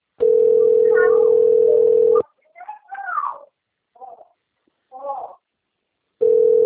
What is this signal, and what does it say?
Ringback tone (North America), 440 + 480 Hz.
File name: rg-1600-unknown- - - .21855